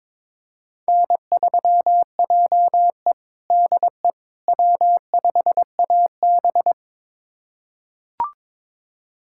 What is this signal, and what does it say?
Morse code, heard as a keyed tone, about 22 wpm, 700 Hz. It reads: N3JE DE W5AB
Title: USA Callsign DE Callsign - 4 Characters - 22wpm